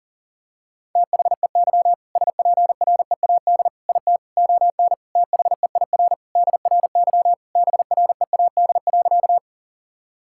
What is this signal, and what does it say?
Morse code, heard as a keyed tone, about 40 wpm, 700 Hz. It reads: THEY SPREAD IT ON THEIR DRY BREAD.
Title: Sentences from Top 1600 Words 40wpm